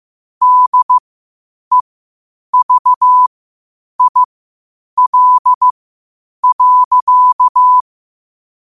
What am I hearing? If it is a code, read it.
Morse code, heard as a keyed tone, about 15 wpm, spaced as for 8 wpm, 1000 Hz. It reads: DEVIL.
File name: qotdfiles / 8wpm